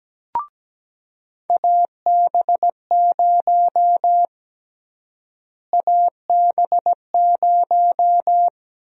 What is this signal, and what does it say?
Morse code, heard as a keyed tone, about 17 wpm, 700 Hz. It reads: AB0 AB0
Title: USA Call Sign Prefixes - Repeated 2x - 17wpm